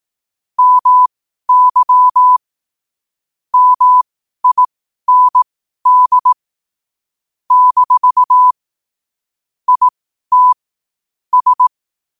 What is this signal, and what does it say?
Morse code, heard as a keyed tone, about 18 wpm, spaced as for 12 wpm, 1000 Hz. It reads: MY MIND - IT S
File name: qotdfiles / 12wpm